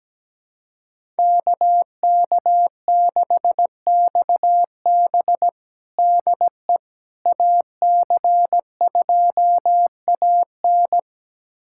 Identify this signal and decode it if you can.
Morse code, heard as a keyed tone, about 17 wpm, 700 Hz. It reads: KK6XB DE AC2AN
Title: USA Callsign DE Callsign - 5 Characters - 17wpm